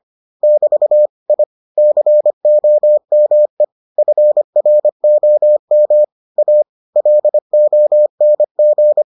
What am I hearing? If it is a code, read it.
Morse code, heard as a keyed tone, about 25 wpm, 600 Hz. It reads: <BT> I COME FROM A LONG